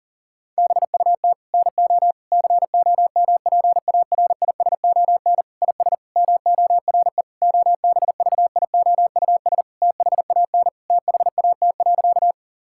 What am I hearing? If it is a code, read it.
Morse code, heard as a keyed tone, about 37 wpm, 700 Hz. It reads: BUT NO COMPARISON IS MORE OBVIOUS THAN THAT.
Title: Sentences from Top 1600 Words 40wpm